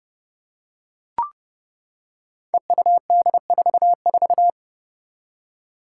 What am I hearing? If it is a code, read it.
Morse code, heard as a keyed tone, about 30 wpm, 700 Hz. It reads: EUD44
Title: Five Challenging Letter-Number Combinations 30wpm